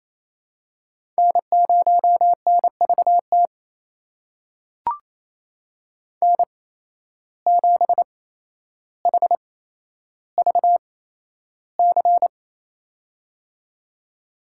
Morse code, heard as a keyed tone, about 28 wpm, spaced as for 12 wpm, 700 Hz. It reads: N0NVT N7HVC
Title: USA Call Signs - 5 Characters - 8x Character Spacing - 28wpm